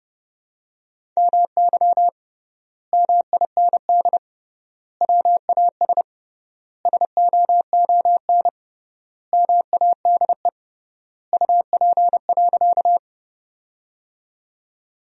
Morse code, heard as a keyed tone, about 30 wpm, 700 Hz. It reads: MY MIND WAS SOON MADE UP.
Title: 3x Word-Spacing - Sentences from Top 500 Words - 30wpm